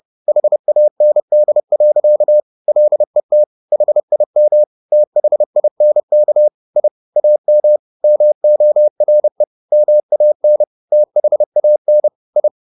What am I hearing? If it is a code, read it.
Morse code, heard as a keyed tone, about 30 wpm, 600 Hz. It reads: HAND. LET HIM THINK I AM MORE MAN THAN I
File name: the old man and the sea / the old man and the sea 1 30wpm